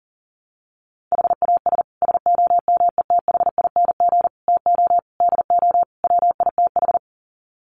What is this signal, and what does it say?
Morse code, heard as a keyed tone, about 40 wpm, 700 Hz. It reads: HAS SOMETHING TO DO WITH
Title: Sets of 5 Words Encore 40wpm